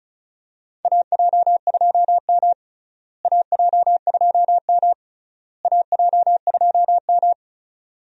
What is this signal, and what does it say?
Morse code, heard as a keyed tone, about 34 wpm, 700 Hz. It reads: AJ2M AJ2M AJ2M
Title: USA Call Signs - 4 Characters - Repeated 3x - 35wpm